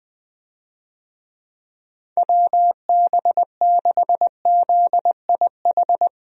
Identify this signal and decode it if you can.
Morse code, heard as a keyed tone, about 21 wpm, 700 Hz. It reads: WB6ZIH
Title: USA Call Signs - 6 Characters - 20wpm